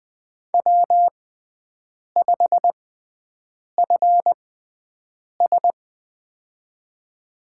Morse code, heard as a keyed tone, about 21 wpm, spaced as for 7 wpm, 700 Hz. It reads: W5FS
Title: USA Call Signs - 6x Character Spacing - 20wpm